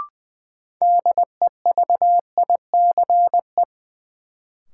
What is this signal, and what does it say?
Morse code, heard as a keyed tone, about 20 wpm, 700 Hz. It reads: DEVICE
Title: Sentences from Top 1500 Words - New Word Drill - 20wpm